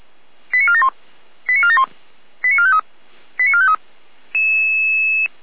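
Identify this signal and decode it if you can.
Five-tone selective call (ZVEI), heard as quick runs of five tones, tones of about 70 ms. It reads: 89471 89471 89453 89453
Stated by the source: ZVEI five-tone call received by SDR on channel 455 (address 89508), recorded by Feuerwehr Orlamünde.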